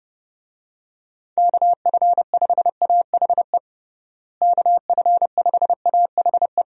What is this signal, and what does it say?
Morse code, heard as a keyed tone, about 32 wpm, 700 Hz. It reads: KF5AHE KF5AHE
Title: USA Call Signs - 6 Characters - Repeated 2x - 30wpm